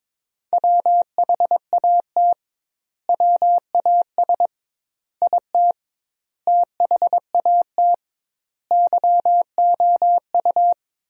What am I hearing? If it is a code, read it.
Morse code, heard as a keyed tone, about 22 wpm, 700 Hz. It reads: WHAT WAS IT THAT YOU
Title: 2x Word-Spacing - Sentences from Top 300 Words - 22wpm